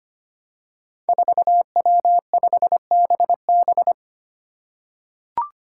Morse code, heard as a keyed tone, about 25 wpm, 700 Hz. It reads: 4W5BB
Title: Five Challenging Letter-Number Combinations 25wpm